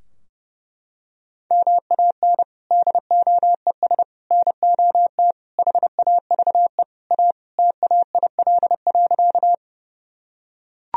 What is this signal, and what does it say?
Morse code, heard as a keyed tone, about 32 wpm, 700 Hz. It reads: MAN DOES NOT HAVE A TAIL.
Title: Sentences from Top 1000 Words - Review Part 3 - 30wpm